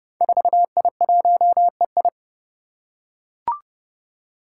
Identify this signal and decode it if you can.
Morse code, heard as a keyed tone, about 29 wpm, 700 Hz. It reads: 4I1EI
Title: Five Challenging Letter-Number Combinations 30wpm